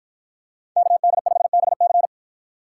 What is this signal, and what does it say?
Morse code, heard as a keyed tone, about 50 wpm, 700 Hz. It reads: KD5BC